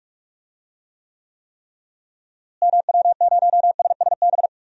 Morse code, heard as a keyed tone, about 46 wpm, 700 Hz. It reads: MW0SSB
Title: DXCC-100 Call Signs - 6 Characters - Repeated 3x - 45wpm